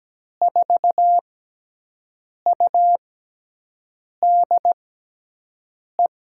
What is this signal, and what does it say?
Morse code, heard as a keyed tone, about 17 wpm, spaced as for 6 wpm, 700 Hz. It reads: 4UDE
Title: USA Call Signs - 6 Characters - 6x Character Spacing - 17wpm